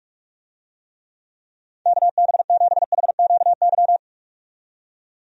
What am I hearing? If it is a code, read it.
Morse code, heard as a keyed tone, about 46 wpm, 700 Hz. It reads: KB7HQY
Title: USA Call Signs - 6 Characters - 45wpm